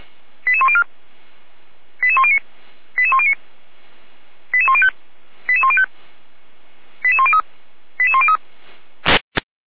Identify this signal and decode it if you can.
Five-tone selective call (ZVEI), heard as quick runs of five tones, tones of about 70 ms. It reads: 88104 88109 88109 88105 88105 88103 88103
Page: SDR/ZVEI